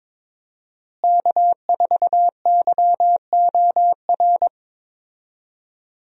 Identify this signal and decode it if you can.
Morse code, heard as a keyed tone, about 21 wpm, 700 Hz. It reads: K4YOR
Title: USA Call Signs - 5 Characters - Repeated 2x - 22wpm